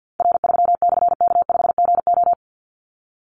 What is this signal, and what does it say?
Morse code, heard as a keyed tone, about 50 wpm, 700 Hz. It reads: R3/K5GO
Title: Call Signs - 50wpm